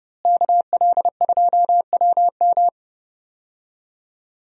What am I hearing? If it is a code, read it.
Morse code, heard as a keyed tone, about 30 wpm, 700 Hz. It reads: KL2WM
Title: USA Call Signs - 5 Characters - Repeated 3x - 30wpm